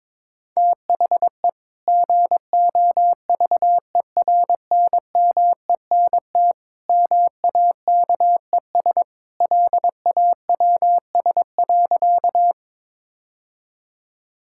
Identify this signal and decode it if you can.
Morse code, heard as a keyed tone, about 22 wpm, 700 Hz. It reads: THE GOVERNMENT MAKES LAWS.
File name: practice / 22wpm / Sentences with 2-Word Phrases - 22wpm